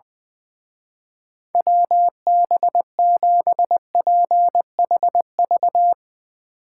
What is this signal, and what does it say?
Morse code, heard as a keyed tone, about 20 wpm, 700 Hz. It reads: WB7PHV
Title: Speed Racing - USA 2x3 Call Signs 20wpm